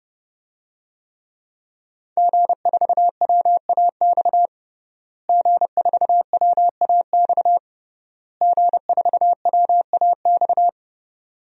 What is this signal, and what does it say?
Morse code, heard as a keyed tone, about 32 wpm, 700 Hz. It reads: G4WAX G4WAX G4WAX
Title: DXCC-100 Call Signs - 5 Characters - Repeated 3x - 30wpm